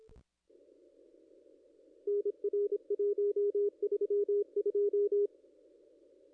Morse code, heard as a keyed tone, about 26 wpm, 410 Hz. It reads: NR 132
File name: contests / ss-cw-k5tr-op-n5rz -1900